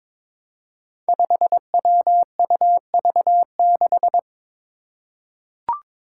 Morse code, heard as a keyed tone, about 23 wpm, 700 Hz. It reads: 5WUV6
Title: Five Challenging Letter-Number Combinations 22wpm